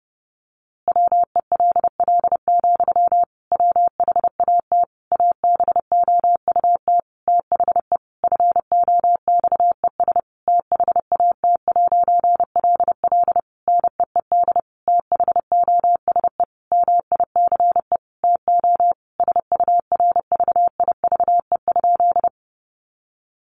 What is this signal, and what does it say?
Morse code, heard as a keyed tone, about 30 wpm, 700 Hz. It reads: WELL, WHAT ABOUT THE FOXES THAT'LL NEED THOSE MICE TO SURVIVE?
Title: A Sound Of Thunder 30wpm